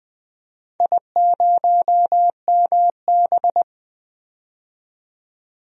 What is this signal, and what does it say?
Morse code, heard as a keyed tone, about 21 wpm, 700 Hz. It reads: I0MB
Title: DXCC-100 Call Signs - 4 Characters - 20wpm